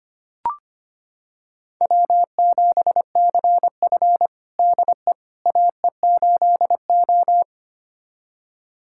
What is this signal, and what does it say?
Morse code, heard as a keyed tone, about 25 wpm, 700 Hz. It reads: W7CF DE AE8O
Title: USA Callsign DE Callsign - 4 Characters - 25wpm